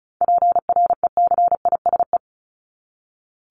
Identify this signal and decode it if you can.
Morse code, heard as a keyed tone, about 34 wpm, 700 Hz. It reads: PRECISE
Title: LY Suffix - Top 100 Words with Base Words 35wpm